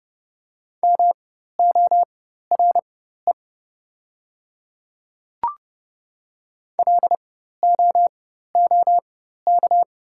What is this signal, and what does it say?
Morse code, heard as a keyed tone, about 30 wpm, spaced as for 14 wpm, 700 Hz. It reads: MORE LOOK
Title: 4x Character Spacing - Top 100 Words - 30wpm